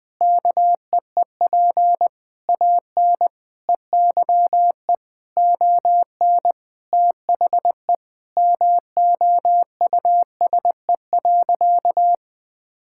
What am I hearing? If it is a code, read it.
Morse code, heard as a keyed tone, about 21 wpm, 700 Hz. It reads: KEEP AN EYE ON THE MOUSE.
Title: All-Age Appropriate Jokes - 20wpm